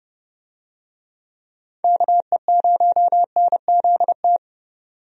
Morse code, heard as a keyed tone, about 32 wpm, 700 Hz. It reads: KE0NZT
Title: USA Call Signs - 6 Characters - 30wpm